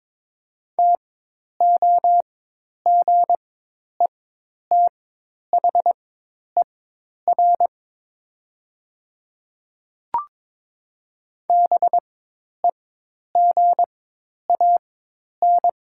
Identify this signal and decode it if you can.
Morse code, heard as a keyed tone, about 22 wpm, spaced as for 10 wpm, 700 Hz. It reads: TOGETHER BEGAN
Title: 4x Character Spacing - Top 300 Words - 22wpm